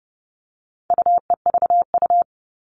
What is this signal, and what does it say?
Morse code, heard as a keyed tone, about 30 wpm, 700 Hz. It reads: UEVU
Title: All Cut Numbers - 4-Digit 30wpm